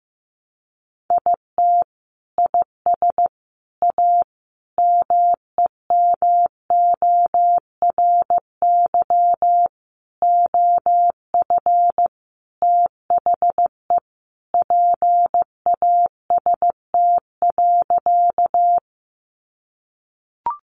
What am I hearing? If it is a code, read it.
Morse code, heard as a keyed tone, about 15 wpm, 700 Hz. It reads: IT IS A MEMORY OF THE PAST.
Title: Sentences from Top 1300 Words 15wpm